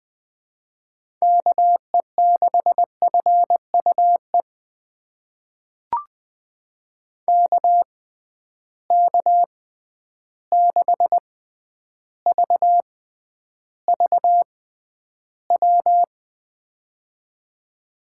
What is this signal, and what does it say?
Morse code, heard as a keyed tone, about 20 wpm, spaced as for 10 wpm, 700 Hz. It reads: KE6FUE KK6VVW
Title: USA Call Signs - 6 Characters - 6x Character Spacing - 20wpm